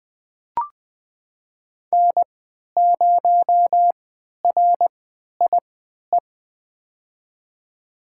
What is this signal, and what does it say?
Morse code, heard as a keyed tone, about 20 wpm, spaced as for 11 wpm, 700 Hz. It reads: N0RIE
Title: USA Call Signs - 5 Characters - 3x Character Spacing - 20wpm